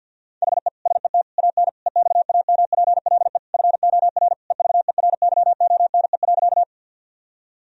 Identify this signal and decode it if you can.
Morse code, heard as a keyed tone, about 48 wpm, 700 Hz. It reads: HE SET AN EXAMPLE FOR EVERYONE.
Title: Sentences from Top 1000 Words - Review Part 1 - 50wpm